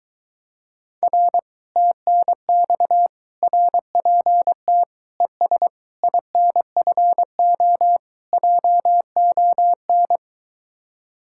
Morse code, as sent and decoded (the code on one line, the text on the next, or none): .-. / - -. -..- / .-. .--. - / . ... / .. -. ..-. --- / .--- --- -.
R TNX RPT ES INFO JON